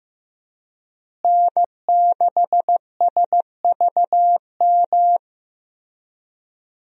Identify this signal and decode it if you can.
Morse code, heard as a keyed tone, about 15 wpm, 700 Hz. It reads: N6SVM
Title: USA Call Signs - 5 Characters - Repeated 3x - 15wpm